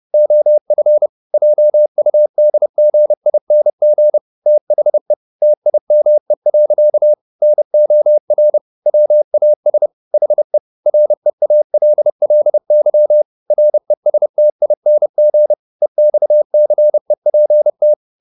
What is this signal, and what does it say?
Morse code, heard as a keyed tone, about 30 wpm, 600 Hz. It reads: OF JUDGING THE TIME. NOR WAS HE REALLY RESTING EXCEPT